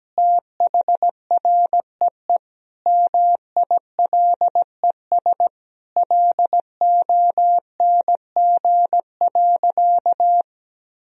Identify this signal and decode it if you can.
Morse code, heard as a keyed tone, about 17 wpm, 700 Hz. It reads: THREE MILES LONG.
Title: Sentences from Top 1000 Words - Review Part 3 - 17wpm